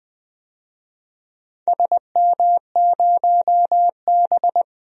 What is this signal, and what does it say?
Morse code, heard as a keyed tone, about 20 wpm, 700 Hz. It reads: SM0B
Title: DXCC-100 Call Signs - 4 Characters - 20wpm